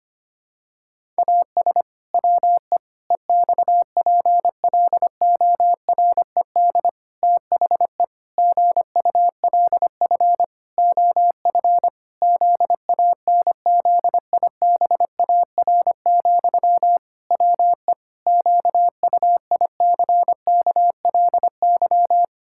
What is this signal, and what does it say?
Morse code, heard as a keyed tone, about 25 wpm, 700 Hz. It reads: AS WE EXPLORED THE GULF OF ZANZIBAR, WE QUICKLY
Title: Pangram Challenge v2 - 25wpm